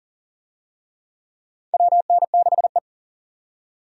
Morse code, heard as a keyed tone, about 39 wpm, 690 Hz. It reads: WN6E
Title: USA Call Signs - 4 Characters - 40wpm